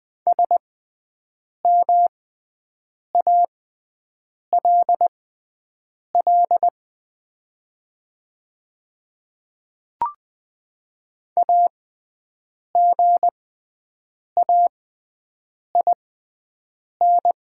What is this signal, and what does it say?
Morse code, heard as a keyed tone, about 20 wpm, spaced as for 7 wpm, 700 Hz. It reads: SMALL AGAIN
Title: 6x Character Spacing - Top 200 Words - 20wpm